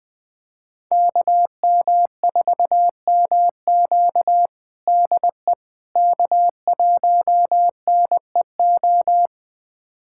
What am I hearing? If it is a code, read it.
Morse code, heard as a keyed tone, about 20 wpm, 700 Hz. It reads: KM4MQ DE K1NEO
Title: USA Callsign DE Callsign - 5 Characters - 20wpm